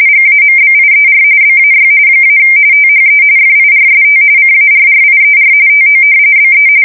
RTTY, about 45.45 baud. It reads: GAMBIER BAY - THE FIRST CARRIER TO BE
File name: fsk170 / battleshipnj